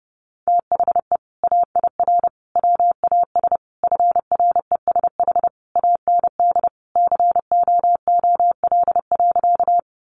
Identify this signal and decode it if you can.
Morse code, heard as a keyed tone, about 28 wpm, 700 Hz. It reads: THE AIR WAS FRESH AND COOL.
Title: Sentences from Top 1200 Words 30wpm